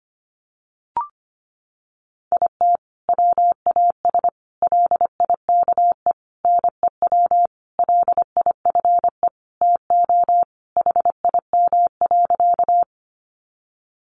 Morse code, heard as a keyed tone, about 25 wpm, 700 Hz. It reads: IT WAS LIKE NEW LIFE TO HIM.
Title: Sentences from Top 300 Words 25wpm